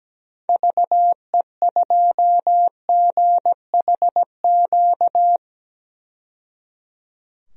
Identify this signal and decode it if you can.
Morse code, heard as a keyed tone, about 17 wpm, 700 Hz. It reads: VE2GHQ
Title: DXCC-100 Call Signs - 6 Characters - 17wpm